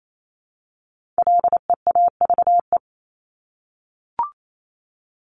Morse code, heard as a keyed tone, about 28 wpm, 700 Hz. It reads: LEAVE
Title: Top 500 Words Repeated 28wpm